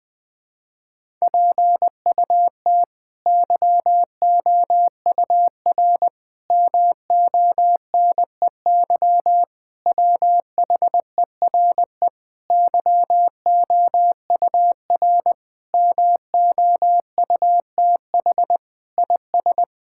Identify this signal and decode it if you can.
Morse code, heard as a keyed tone, about 21 wpm, 700 Hz. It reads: PUT YOUR MONEY WHERE YOUR MOUTH IS
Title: Sets of 8 Words - 20wpm